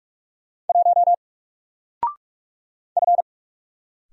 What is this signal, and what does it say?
Morse code, heard as a keyed tone, about 46 wpm, 700 Hz. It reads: 1 F
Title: Single Letter-Number Alt 45wpm